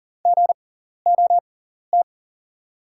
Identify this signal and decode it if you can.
Morse code, heard as a keyed tone, about 38 wpm, spaced as for 14 wpm, 700 Hz. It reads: GOT
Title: 6x Character Spacing - Top 300 Words - 40wpm